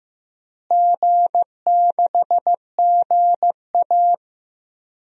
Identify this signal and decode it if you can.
Morse code, heard as a keyed tone, about 15 wpm, 700 Hz. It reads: G6GA
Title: DXCC-100 Call Signs - 4 Characters - Repeated 3x - 15wpm